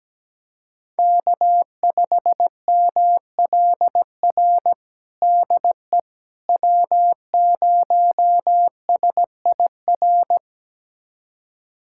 Morse code, heard as a keyed tone, about 17 wpm, 700 Hz. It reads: K5MLR DE W0SIR
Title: USA Callsign DE Callsign - 5 Characters - 17wpm